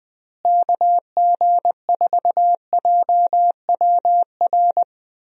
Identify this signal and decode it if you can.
Morse code, heard as a keyed tone, about 20 wpm, 700 Hz. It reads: KG4JWR
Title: USA Call Signs - 6 Characters - Repeated 2x - 20wpm